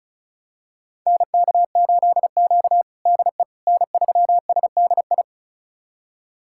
Morse code, heard as a keyed tone, about 34 wpm, 700 Hz. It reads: NK8Q DE N3SDI
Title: USA Callsign DE Callsign 35wpm